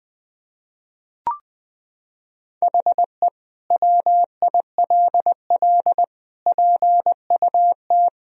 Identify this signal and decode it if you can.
Morse code, heard as a keyed tone, about 21 wpm, 700 Hz. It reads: HE WILL PUT
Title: Sentences from Top 1000 Words - Review Part 2 - 20wpm